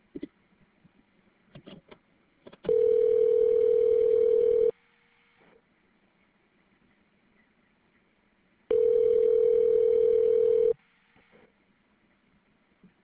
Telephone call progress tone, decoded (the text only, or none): ringback tone